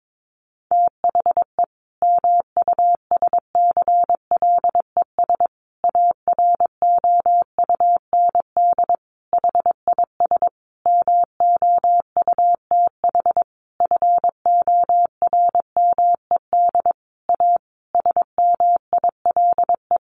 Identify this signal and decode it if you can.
Morse code, heard as a keyed tone, about 22 wpm, 700 Hz. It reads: THE MUSCLES AROUND HIS MOUTH FORMED A SMILE
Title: A Sound Of Thunder 22wpm